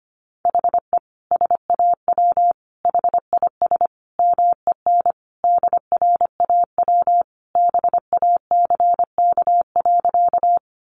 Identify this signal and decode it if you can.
Morse code, heard as a keyed tone, about 25 wpm, 700 Hz. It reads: HE SAW HIS MEN DRAW BACK.
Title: Sentences from Top 300 Words 25wpm